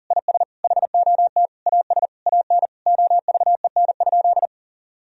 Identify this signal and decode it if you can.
Morse code, heard as a keyed tone, about 37 wpm, 700 Hz. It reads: IS HOT AS AN OVEN?
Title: Sentences from Top 1000 Words - Review Part 4 - 40wpm